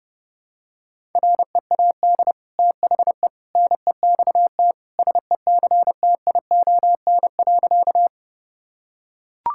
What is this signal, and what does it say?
Morse code, heard as a keyed tone, about 32 wpm, 700 Hz. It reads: READ THE NEXT SECTION.
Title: Sentences from Top 1000 Words - Review Part 5 - 30wpm